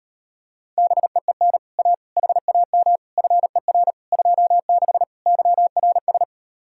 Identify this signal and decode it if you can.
Morse code, heard as a keyed tone, about 33 wpm, 700 Hz. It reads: BEEN A HAM FER 26 YRS